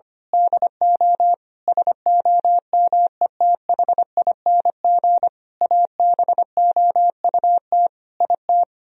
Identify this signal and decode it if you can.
Morse code, heard as a keyed tone, about 24 wpm, 700 Hz. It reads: DO SOMETHING ABOUT IT
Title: Speed-Racing - Sets of 5 Words 25wpm